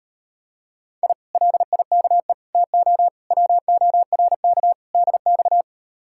Morse code, heard as a keyed tone, about 38 wpm, 700 Hz. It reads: I LIKE TO WORK DX